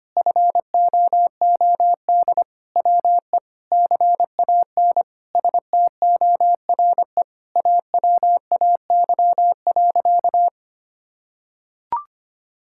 Morse code, heard as a keyed tone, about 25 wpm, 700 Hz. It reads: FOOD WE CAN STORE AWAY.
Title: Sentences from Top 1000 Words - Review Part 5 - 25wpm